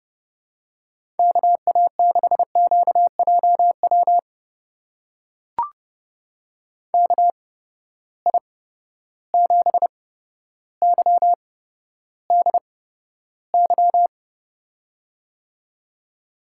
Morse code, heard as a keyed tone, about 30 wpm, spaced as for 13 wpm, 700 Hz. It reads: KA6QJW KI7YDY
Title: USA Call Signs - 6 Characters - 8x Character Spacing - 30wpm